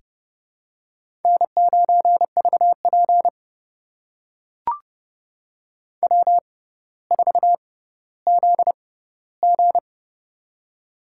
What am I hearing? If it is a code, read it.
Morse code, heard as a keyed tone, about 30 wpm, spaced as for 15 wpm, 700 Hz. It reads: N9VP W4ZG
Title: USA Call Signs - 4 Characters - 6x Character Spacing - 30wpm